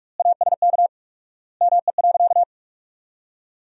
Morse code, heard as a keyed tone, about 47 wpm, 700 Hz. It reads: ASK ME.
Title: 4x Word-Spacing - Sentences from Top 500 Words - 45wpm